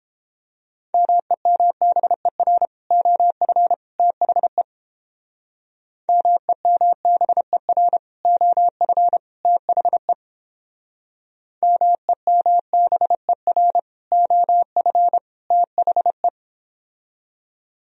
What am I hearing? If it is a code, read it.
Morse code, heard as a keyed tone, about 29 wpm, 700 Hz. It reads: MEMBER OF THE MEMBER OF THE MEMBER OF THE